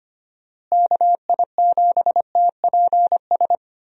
Morse code, heard as a keyed tone, about 25 wpm, 700 Hz. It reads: KI7TPS
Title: USA Call Signs - 6 Characters - Repeated 2x - 25wpm